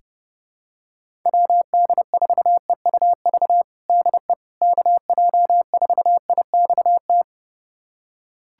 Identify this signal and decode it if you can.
Morse code, heard as a keyed tone, about 30 wpm, 700 Hz. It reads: WD4EUV DE KJ4IXT
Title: USA Callsign DE Callsign - 6 Characters - 30wpm